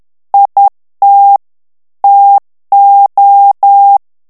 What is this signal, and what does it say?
Morse code, heard as a keyed tone, about 10 wpm, 800 Hz. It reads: IT TO
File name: ANGLAIS texte clair 3 vitesse 10 mots minute